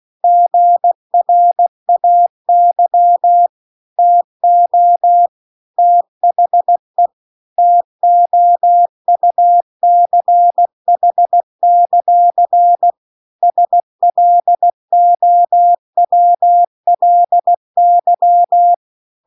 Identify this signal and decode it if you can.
Morse code, heard as a keyed tone, about 16 wpm, 700 Hz. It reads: GRAY TO THE TOUCH; SLOWLY
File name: CW-The-Big-Twoheaded-River-2-16WPM